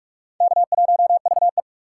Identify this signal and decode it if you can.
Morse code, heard as a keyed tone, about 46 wpm, 690 Hz. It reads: K1VE
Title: USA Call Signs - 4 Characters - 3x Character Spacing - 45wpm